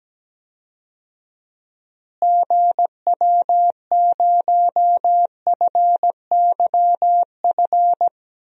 Morse code, heard as a keyed tone, about 17 wpm, 700 Hz. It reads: GW0FYF
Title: DXCC-100 Call Signs - 6 Characters - Repeated 3x - 17wpm